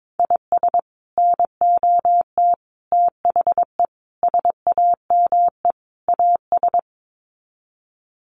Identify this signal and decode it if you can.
Morse code, heard as a keyed tone, about 22 wpm, 700 Hz. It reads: IS NOT THE SAME AS
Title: Sets of 5 Words Encore 22wpm